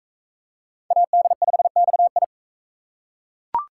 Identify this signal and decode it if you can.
Morse code, heard as a keyed tone, about 38 wpm, 700 Hz. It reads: AD5XI